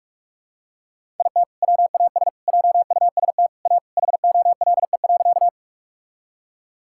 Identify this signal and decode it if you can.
Morse code, heard as a keyed tone, about 46 wpm, 700 Hz. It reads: IT WAS JUST A HOLE.
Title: Sentences from Top 1600 Words 45wpm